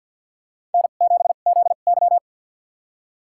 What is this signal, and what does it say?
Morse code, heard as a keyed tone, about 53 wpm, spaced as for 37 wpm, 690 Hz. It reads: N7ZY
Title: USA Call Signs - 2x Character Spacing - 50wpm